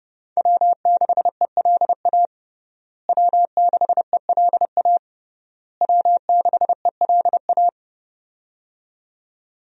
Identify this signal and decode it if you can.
Morse code, heard as a keyed tone, about 28 wpm, 700 Hz. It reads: W6ELA W6ELA W6ELA
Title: USA Call Signs - 5 Characters - Repeated 3x - 30wpm